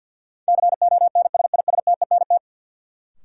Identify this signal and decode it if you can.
Morse code, heard as a keyed tone, about 49 wpm, 700 Hz. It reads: CONSISTENT
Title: Sentences from Top 1500 Words - New Word Drill Alt - 50wpm